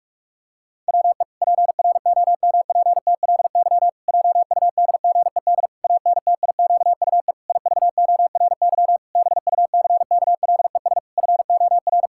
Morse code, heard as a keyed tone, about 46 wpm, 700 Hz. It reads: WE PROMPTLY JUDGED ANTIQUE IVORY BUCKLES FOR
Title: Pangram Challenge v2 - 45wpm